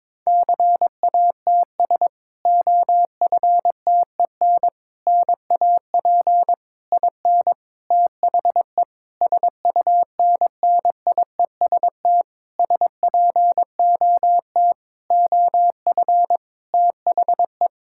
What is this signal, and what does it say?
Morse code, heard as a keyed tone, about 22 wpm, 700 Hz. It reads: CATS OFTEN NAP IN THE SUNNIEST SPOT OF THE
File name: Difficult Sentences with Hints - 22wpm